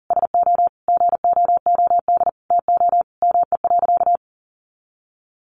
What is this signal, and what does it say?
Morse code, heard as a keyed tone, about 40 wpm, 700 Hz. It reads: SO GOOD TO ME.
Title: Sentences from Top 1400 Words 40wpm